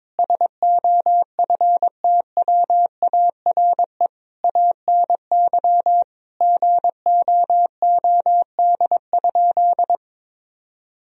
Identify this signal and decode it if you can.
Morse code, heard as a keyed tone, about 21 wpm, 700 Hz. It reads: SOFTWARE ANY GOOD?
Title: Sentences from Top 1000 Words - Review Part 3 - 22wpm